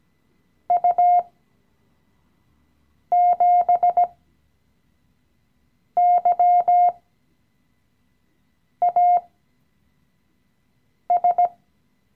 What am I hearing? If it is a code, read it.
Morse code, heard as a keyed tone, about 17 wpm, spaced as for 4 wpm, 710 Hz. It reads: U7YAS
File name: CW-Online-Kurs / Anfänger / Lektion 016